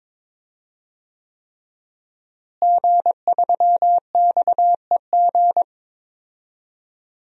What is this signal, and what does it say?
Morse code, heard as a keyed tone, about 22 wpm, 700 Hz. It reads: G3XEG
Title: DXCC-100 Call Signs - 5 Characters - Repeated 3x - 22wpm